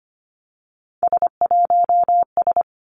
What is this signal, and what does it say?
Morse code, heard as a keyed tone, about 25 wpm, 700 Hz. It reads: S1S
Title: Three Challenging Letter-Number Combinations 25wpm